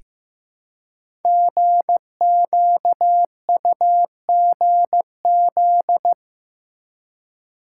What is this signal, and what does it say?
Morse code, heard as a keyed tone, about 15 wpm, 700 Hz. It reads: GQUGZ
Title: Code Groups - 5 Letters - 15wpm